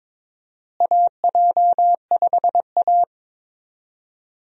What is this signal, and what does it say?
Morse code, heard as a keyed tone, about 22 wpm, 700 Hz. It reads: AJ5A
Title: USA Call Signs - 4 Characters - Repeated 2x - 22wpm